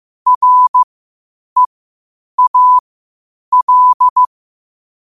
Morse code, heard as a keyed tone, about 15 wpm, spaced as for 8 wpm, 1000 Hz. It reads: REAL
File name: qotdfiles / 8wpm